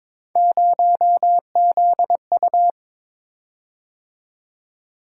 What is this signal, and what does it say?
Morse code, heard as a keyed tone, about 21 wpm, 700 Hz. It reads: 0ZU
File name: practice / 22wpm / Three Letter-Number Combinations - 22wpm